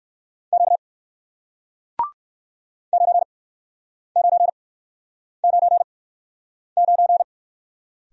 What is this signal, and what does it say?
Morse code, heard as a keyed tone, about 61 wpm, spaced as for 9 wpm, 700 Hz. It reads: Q 9999